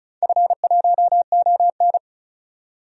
Morse code, heard as a keyed tone, about 34 wpm, 700 Hz. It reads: F1ON
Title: DXCC-100 Call Signs - 4 Characters - 35wpm